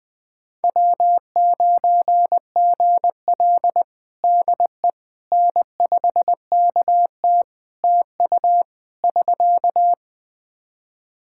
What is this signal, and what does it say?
Morse code, heard as a keyed tone, about 20 wpm, 700 Hz. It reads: W9GL DE N5KT TU <SK>
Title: Practice QSO - Exchange 4 - Ending - 20wpm